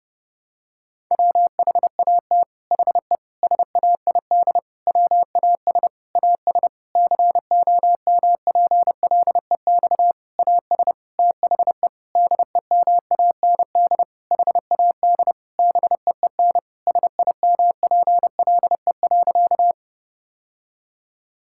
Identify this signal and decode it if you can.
Morse code, heard as a keyed tone, about 32 wpm, 700 Hz. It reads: WHAT HE SAID WAS AS COMPLEX AS THE DEMAND HAD BEEN SIMPLE.
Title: Sentences from Top 1000 Words - Review Part 6 - 30wpm